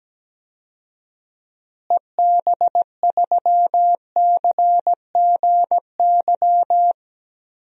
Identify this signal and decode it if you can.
Morse code, heard as a keyed tone, about 17 wpm, 700 Hz. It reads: EB3CGY
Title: DXCC-100 Call Signs - 6 Characters - Repeated 3x - 17wpm